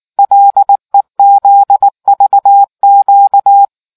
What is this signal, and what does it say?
Morse code, heard as a keyed tone, about 19 wpm, 800 Hz. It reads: LEZVQ